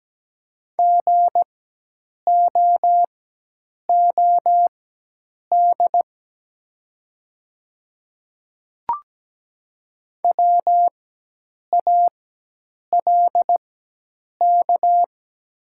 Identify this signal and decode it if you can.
Morse code, heard as a keyed tone, about 17 wpm, spaced as for 8 wpm, 700 Hz. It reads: GOOD WALK
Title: 4x Character Spacing - Top 300 Words - 17wpm